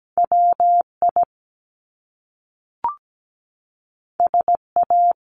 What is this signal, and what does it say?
Morse code, heard as a keyed tone, about 17 wpm, 700 Hz. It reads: WI SA
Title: Top Two Letter Combinations in Words 17wpm